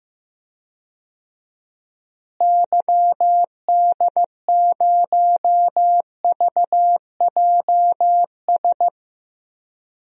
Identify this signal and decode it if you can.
Morse code, heard as a keyed tone, about 15 wpm, 700 Hz. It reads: YD0VJS
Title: DXCC-100 Call Signs - 6 Characters - Repeated 2x - 15wpm